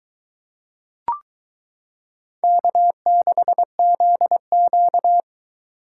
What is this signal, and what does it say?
Morse code, heard as a keyed tone, about 22 wpm, 700 Hz. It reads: K6ZQ